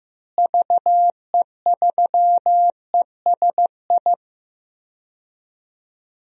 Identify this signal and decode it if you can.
Morse code, heard as a keyed tone, about 16 wpm, 700 Hz. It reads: VE3ESI
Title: DXCC-100 Call Signs - 6 Characters - Repeated 2x - 15wpm